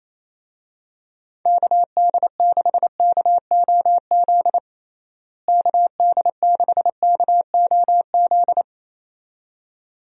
Morse code, heard as a keyed tone, about 28 wpm, 700 Hz. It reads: KD6KOZ KD6KOZ
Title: USA Call Signs - 6 Characters - Repeated 2x - 28wpm